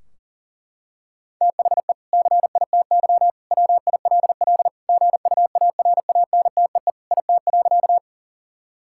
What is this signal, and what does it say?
Morse code, heard as a keyed tone, about 40 wpm, 700 Hz. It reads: THE CITY WILL GUARANTEE IT.
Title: Sentences from Top 1600 Words 40wpm